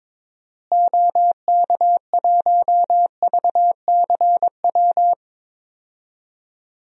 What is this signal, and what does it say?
Morse code, heard as a keyed tone, about 22 wpm, 700 Hz. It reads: OK1VCW
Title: DXCC-100 Call Signs - 6 Characters - Repeated 3x - 22wpm